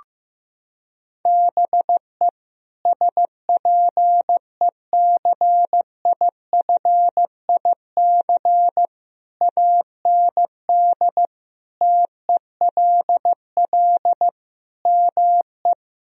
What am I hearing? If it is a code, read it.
Morse code, heard as a keyed tone, about 15 wpm, 700 Hz. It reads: BE SPECIFIC AND TELL ME
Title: Sentences from Top 1000 Words - Review Part 3 - 15wpm